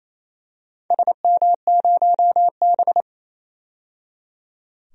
Morse code, heard as a keyed tone, about 28 wpm, 700 Hz. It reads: SM0B
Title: DXCC-100 Call Signs - 4 Characters - 28wpm